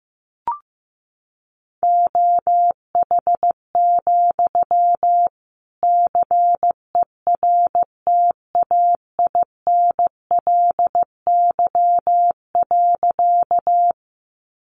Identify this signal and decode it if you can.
Morse code, heard as a keyed tone, about 15 wpm, 700 Hz. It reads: OH, CERTAINLY.
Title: Sentences from Top 900 Words 15wpm